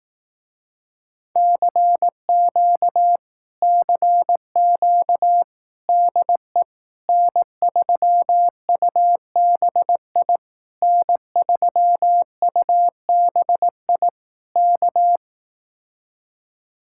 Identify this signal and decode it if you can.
Morse code, heard as a keyed tone, about 18 wpm, 700 Hz. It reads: CQ CQ DE N3UBI N3UBI K
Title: Speed-Racing Common QSO Phrases Encore - 15wpm